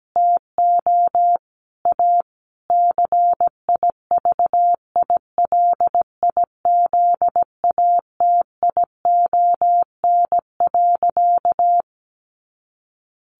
Morse code, heard as a keyed tone, about 17 wpm, 700 Hz. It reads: TO A CIVILIZATION.
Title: A Sound Of Thunder 17wpm